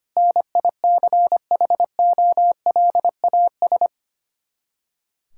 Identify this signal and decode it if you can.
Morse code, heard as a keyed tone, about 25 wpm, 700 Hz. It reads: NICHOLAS
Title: K1USN SST QSO Practice 25wpm